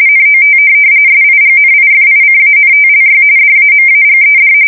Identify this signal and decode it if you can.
RTTY, about 45.45 baud. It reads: NOT SIGNIFICANTLY DAMAGED